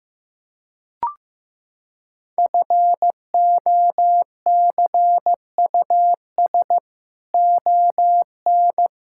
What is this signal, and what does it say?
Morse code, heard as a keyed tone, about 16 wpm, 700 Hz. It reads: FOCUS ON
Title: Sentences from Top 1000 Words - Review Part 6 - 15wpm